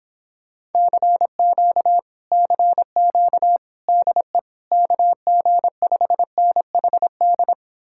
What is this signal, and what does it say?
Morse code, heard as a keyed tone, about 29 wpm, 700 Hz. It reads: CQ CQ DE KG5NHD